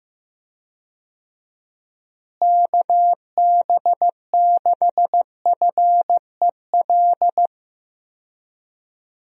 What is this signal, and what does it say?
Morse code, heard as a keyed tone, about 15 wpm, 700 Hz. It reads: KB6FEL
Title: USA Call Signs - 6 Characters - Repeated 2x - 15wpm